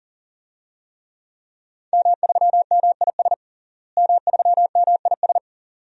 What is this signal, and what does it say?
Morse code, heard as a keyed tone, about 40 wpm, 700 Hz. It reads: M3MIS M3MIS
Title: DXCC-100 Call Signs - 5 Characters - Repeated 2x - 40wpm